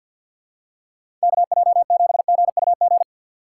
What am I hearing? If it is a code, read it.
Morse code, heard as a keyed tone, about 48 wpm, 700 Hz. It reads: K J 7 G U G
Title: USA Call Signs - 6 Characters - 6x Character Spacing - 50wpm